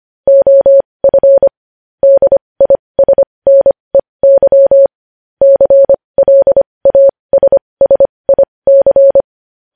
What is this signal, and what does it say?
Morse code, heard as a keyed tone, about 25 wpm, spaced as for 20 wpm, 550 Hz. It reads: OF DISNEY CLASSIC
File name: podcast / News headlines-1x-25x20